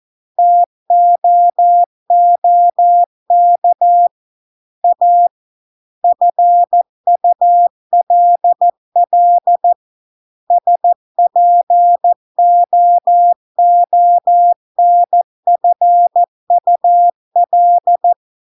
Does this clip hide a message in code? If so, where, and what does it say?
Yes: Morse code from 0.4 to 18.1 s, heard as a keyed tone, about 14 wpm, 700 Hz. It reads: TOOK A FULL SPOONFUL